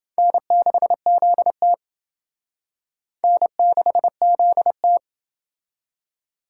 Morse code, heard as a keyed tone, about 29 wpm, 700 Hz. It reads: N6ZT N6ZT